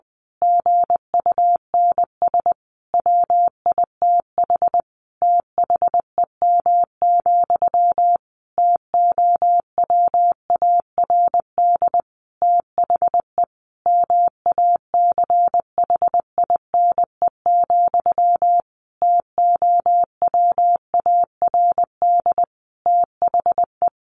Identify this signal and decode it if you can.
Morse code, heard as a keyed tone, about 20 wpm, 700 Hz. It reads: GUNS WITH THEM, TOWARD THE MACHINE, TOWARD THE
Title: A Sound Of Thunder 20wpm